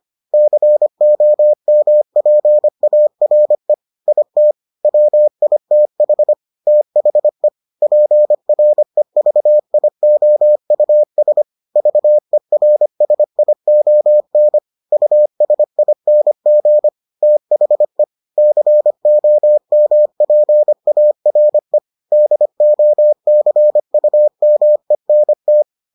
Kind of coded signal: Morse code

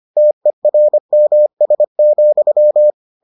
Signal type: Morse code

